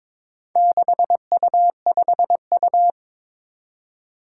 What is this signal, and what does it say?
Morse code, heard as a keyed tone, about 22 wpm, 700 Hz. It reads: 6U5U
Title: Four Challenging Letter-Number Combinations 22wpm